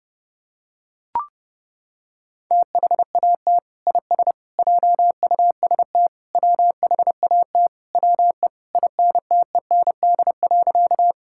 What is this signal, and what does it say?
Morse code, heard as a keyed tone, about 32 wpm, 700 Hz. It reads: THAT IS JUST WHAT WE INTEND.
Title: Sentences from Top 1600 Words 30wpm